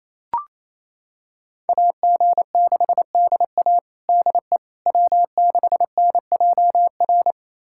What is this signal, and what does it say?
Morse code, heard as a keyed tone, about 27 wpm, 700 Hz. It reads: AG6DA DE W6NJR
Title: USA Callsign DE Callsign - 5 Characters - 28wpm